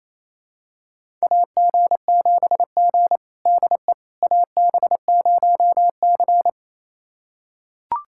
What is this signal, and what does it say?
Morse code, heard as a keyed tone, about 28 wpm, 700 Hz. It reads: AG7G DE AB0C
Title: USA Callsign DE Callsign - 4 Characters - 28wpm